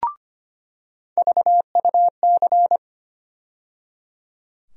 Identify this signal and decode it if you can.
Morse code, heard as a keyed tone, about 25 wpm, 700 Hz. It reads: VUC
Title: USA Call Sign Postfixes - 3 Letters 25wpm